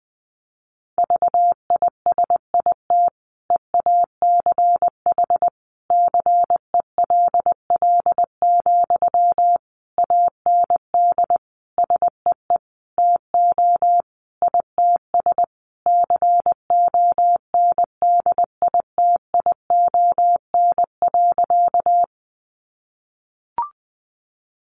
Morse code, heard as a keyed tone, about 20 wpm, 700 Hz. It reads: VISIT EACH CELL, AND SEE TO ITS CONDITION.
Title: Sentences from Top 1100 Words 20wpm